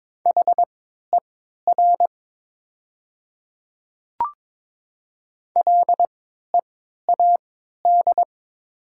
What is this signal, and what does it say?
Morse code, heard as a keyed tone, about 22 wpm, spaced as for 13 wpm, 700 Hz. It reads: HER LEAD
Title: 3x Character Spacing - Top 500 Words - 22wpm